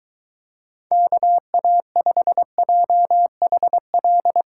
Morse code, heard as a keyed tone, about 23 wpm, 700 Hz. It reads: KA5JHL